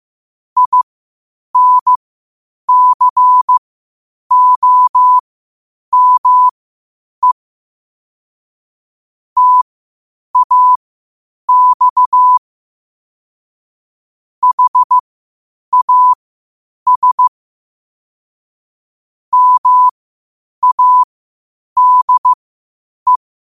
Morse code, heard as a keyed tone, about 15 wpm, spaced as for 8 wpm, 1000 Hz. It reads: INCOME TAX HAS MADE
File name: qotdfiles / 8wpm